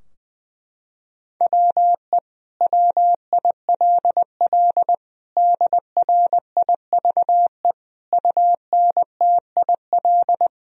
Morse code, heard as a keyed tone, about 21 wpm, 700 Hz. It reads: WE WILL DRIVE UNTIL
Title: Sentences from Top 1000 Words - Review Part 2 - 20wpm